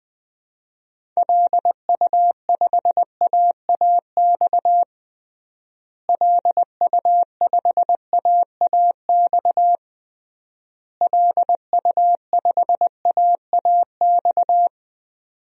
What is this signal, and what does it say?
Morse code, heard as a keyed tone, about 20 wpm, 700 Hz. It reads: LU5AAX LU5AAX LU5AAX
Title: DXCC-100 Call Signs - 6 Characters - Repeated 3x - 20wpm